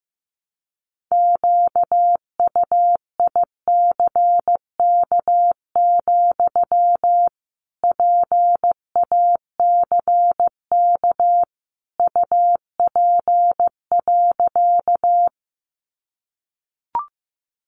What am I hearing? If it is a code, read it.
Morse code, heard as a keyed tone, about 15 wpm, 700 Hz. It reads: QUICK, PACK UP.
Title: Sentences from Top 1300 Words 15wpm